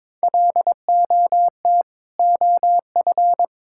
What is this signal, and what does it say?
Morse code, heard as a keyed tone, about 22 wpm, 700 Hz. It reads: LOT OF
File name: Build A Phrase - 2-Word Phrases - 22wpm